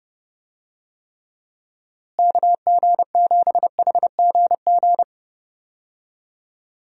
Morse code, heard as a keyed tone, about 32 wpm, 700 Hz. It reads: KG7HGG
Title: USA Call Signs - 6 Characters - Repeated 3x - 30wpm